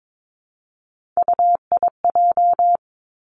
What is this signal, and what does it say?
Morse code, heard as a keyed tone, about 22 wpm, 700 Hz. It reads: UIJ
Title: Three Challenging Letter-Number Combinations 22wpm